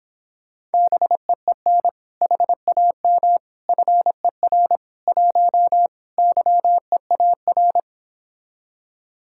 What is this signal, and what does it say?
Morse code, heard as a keyed tone, about 29 wpm, 700 Hz. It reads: BEEN HAM FER 1 YEAR